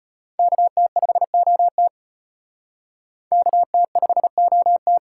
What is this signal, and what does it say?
Morse code, heard as a keyed tone, about 33 wpm, 700 Hz. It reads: KT5OT KT5OT